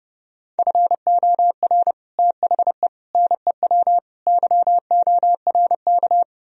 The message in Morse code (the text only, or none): FOR THE NEW YORK